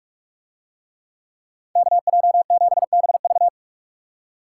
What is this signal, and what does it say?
Morse code, heard as a keyed tone, about 44 wpm, 700 Hz. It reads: KJ7BV